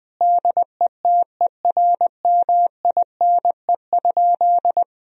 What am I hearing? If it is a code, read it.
Morse code, heard as a keyed tone, about 20 wpm, 700 Hz. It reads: DETERMINE?
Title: Sentences from Top 1000 Words - Review Part 4 - 20wpm